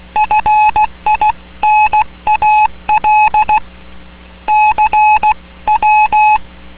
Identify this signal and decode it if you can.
Morse code, heard as a keyed tone, about 16 wpm, 850 Hz. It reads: FINAL CW